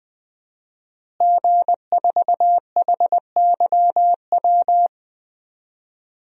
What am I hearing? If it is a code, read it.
Morse code, heard as a keyed tone, about 21 wpm, 700 Hz. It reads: G4HYW
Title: DXCC-100 Call Signs - 5 Characters - Repeated 3x - 20wpm